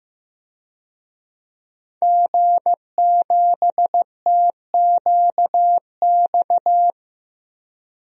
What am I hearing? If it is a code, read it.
Morse code, heard as a keyed tone, about 15 wpm, 700 Hz. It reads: G7TQX
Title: DXCC-100 Call Signs - 5 Characters - Repeated 3x - 15wpm